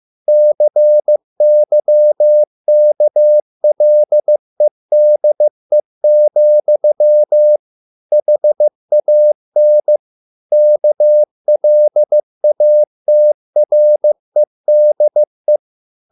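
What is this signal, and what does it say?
Morse code, heard as a keyed tone, about 15 wpm, 600 Hz. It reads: CYKLEDE, HAN KLATREDE
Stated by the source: Morse code at a slow 15 wpm from CW med Gnister.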